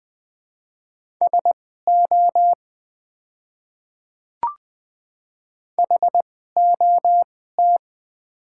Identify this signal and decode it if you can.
Morse code, heard as a keyed tone, about 20 wpm, spaced as for 14 wpm, 690 Hz. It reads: SO HOT
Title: 2x Character Spacing - Top 100 Words - 20wpm